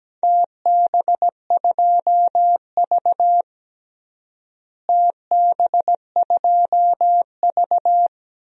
Morse code, heard as a keyed tone, about 17 wpm, 700 Hz. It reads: TB2V TB2V
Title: DXCC-100 Call Signs - 4 Characters - Repeated 3x - 17wpm